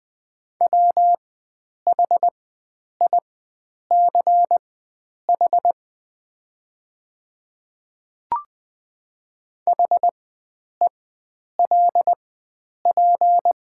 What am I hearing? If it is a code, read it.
Morse code, heard as a keyed tone, about 20 wpm, spaced as for 9 wpm, 700 Hz. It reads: WHICH HELP
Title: 4x Character Spacing - Top 300 Words - 20wpm